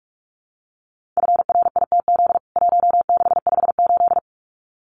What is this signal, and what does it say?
Morse code, heard as a keyed tone, about 44 wpm, 700 Hz. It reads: FRITZ 1658
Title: CWT Exchange 45wpm